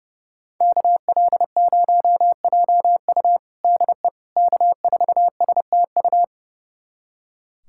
Morse code, heard as a keyed tone, about 29 wpm, 700 Hz. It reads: KL0JU DE K4STU
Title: USA Callsign DE Callsign - 5 Characters - 30wpm